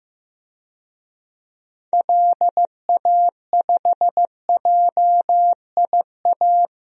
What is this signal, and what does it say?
Morse code, heard as a keyed tone, about 15 wpm, 700 Hz. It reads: LA5JIA
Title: DXCC-100 Call Signs - 6 Characters - Repeated 2x - 15wpm